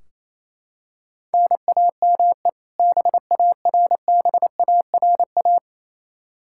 Morse code, heard as a keyed tone, about 28 wpm, 700 Hz. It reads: NAME BARBARA
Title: Practice QSO - Exchange 1 - Beginning - 28wpm